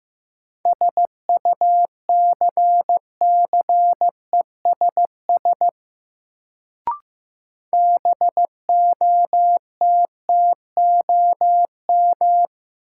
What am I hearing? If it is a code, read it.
Morse code, heard as a keyed tone, about 15 wpm, 700 Hz. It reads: SUCCESS BOTTOM
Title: Sentences from Top 900 Words - New Word Drill - 15wpm